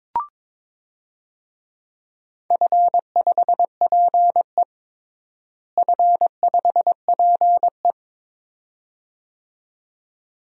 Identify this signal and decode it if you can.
Morse code, heard as a keyed tone, about 21 wpm, 700 Hz. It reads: F5PE F5PE
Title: DXCC-100 Call Signs - 4 Characters - Repeated 2x - 22wpm